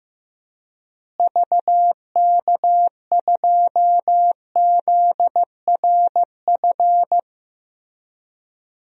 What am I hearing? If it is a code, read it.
Morse code, heard as a keyed tone, about 15 wpm, 700 Hz. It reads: VK2ZRF
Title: DXCC-100 Call Signs - 6 Characters - Repeated 2x - 15wpm